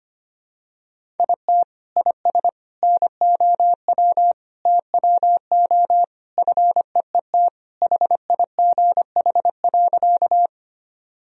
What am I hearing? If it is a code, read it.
Morse code, heard as a keyed tone, about 25 wpm, 700 Hz. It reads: IT IS NOW TWO FEET HIGH.
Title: Sentences from Top 1000 Words - Review Part 1 - 25wpm